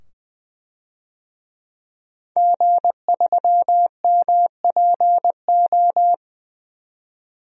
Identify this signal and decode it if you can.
Morse code, heard as a keyed tone, about 20 wpm, 700 Hz. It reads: G3MPO
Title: DXCC-100 Call Signs - 5 Characters - 20wpm